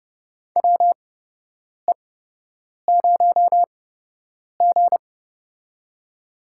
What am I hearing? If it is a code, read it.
Morse code, heard as a keyed tone, about 30 wpm, spaced as for 8 wpm, 700 Hz. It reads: WE0G
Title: USA Call Signs - 4 Characters - 8x Character Spacing - 30wpm